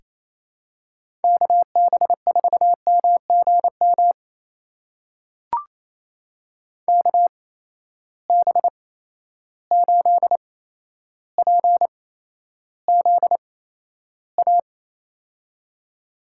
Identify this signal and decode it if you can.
Morse code, heard as a keyed tone, about 28 wpm, spaced as for 12 wpm, 700 Hz. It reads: KB4MGM KB8PZA
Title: USA Call Signs - 6 Characters - 8x Character Spacing - 28wpm